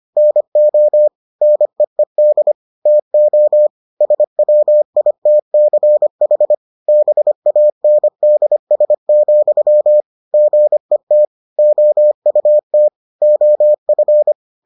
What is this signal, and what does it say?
Morse code, heard as a keyed tone, about 25 wpm, 600 Hz. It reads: NO NEED TO SWITCH BANDS, GET OUT OF